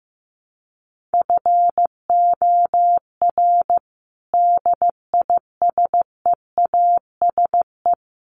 Morse code, heard as a keyed tone, about 15 wpm, 700 Hz. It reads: FOR DISEASE
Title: Sets of 5 Words Encore2 15wpm